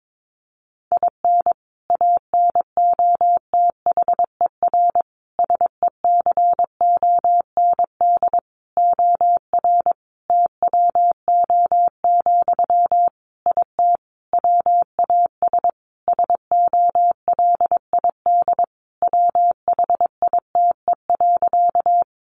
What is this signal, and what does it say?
Morse code, heard as a keyed tone, about 22 wpm, 700 Hz. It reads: IN ANOTHER SECOND OR TWO, IT WAS SOLID WHITE.
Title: Sentences from Top 1500 Words 22wpm